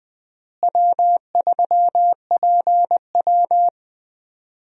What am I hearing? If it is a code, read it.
Morse code, heard as a keyed tone, about 21 wpm, 700 Hz. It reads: W3PW
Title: USA Call Signs - 4 Characters - Repeated 3x - 20wpm